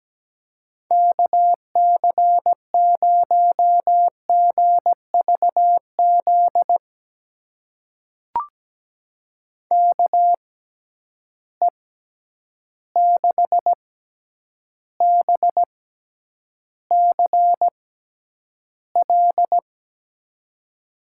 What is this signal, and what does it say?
Morse code, heard as a keyed tone, about 17 wpm, spaced as for 9 wpm, 700 Hz. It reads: KC0GVZ KE6BCL
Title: USA Call Signs - 6 Characters - 6x Character Spacing - 17wpm